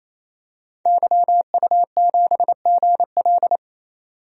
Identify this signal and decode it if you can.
Morse code, heard as a keyed tone, about 28 wpm, 700 Hz. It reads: YU7GL
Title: DXCC-100 Call Signs - 5 Characters - 28wpm